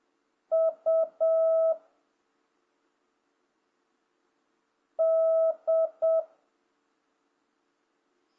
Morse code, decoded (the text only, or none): U D